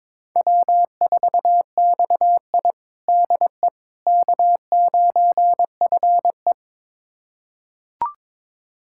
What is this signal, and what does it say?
Morse code, heard as a keyed tone, about 22 wpm, 700 Hz. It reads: W4XI DE K9FE
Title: USA Callsign DE Callsign - 4 Characters - 22wpm